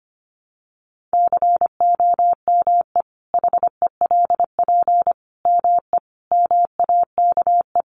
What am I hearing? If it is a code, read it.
Morse code, heard as a keyed tone, about 25 wpm, 700 Hz. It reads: COME HELP ME MAKE
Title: Sentences from Top 1300 Words 25wpm